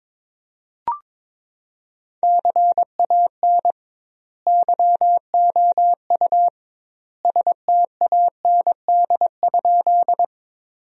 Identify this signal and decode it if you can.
Morse code, heard as a keyed tone, about 22 wpm, 700 Hz. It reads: CAN YOU STAND?
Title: 2x Word-Spacing - Sentences from Top 300 Words - 22wpm